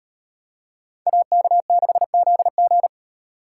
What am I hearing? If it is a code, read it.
Morse code, heard as a keyed tone, about 32 wpm, 700 Hz. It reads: AK6ZG